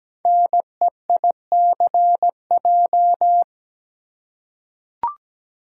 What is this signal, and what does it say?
Morse code, heard as a keyed tone, about 17 wpm, 700 Hz. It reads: NEICJ
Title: Code Groups - 5 Letters - 17wpm